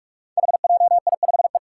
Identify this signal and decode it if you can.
Morse code, heard as a keyed tone, about 44 wpm, 700 Hz. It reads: HJI5E
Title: Five Challenging Letter-Number Combinations 45wpm